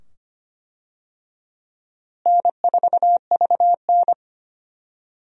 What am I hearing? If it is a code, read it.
Morse code, heard as a keyed tone, about 25 wpm, 700 Hz. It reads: N4VN
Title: USA Call Signs - 4 Characters - Repeated 2x - 25wpm